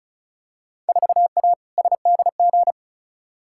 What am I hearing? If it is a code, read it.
Morse code, heard as a keyed tone, about 36 wpm, 700 Hz. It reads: 4A SDG
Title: Field Day Exchanges - 35wpm